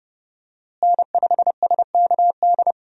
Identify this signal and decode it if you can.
Morse code, heard as a keyed tone, about 30 wpm, 700 Hz. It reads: N5SKD
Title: USA Call Signs - 5 Characters - Repeated 2x - 30wpm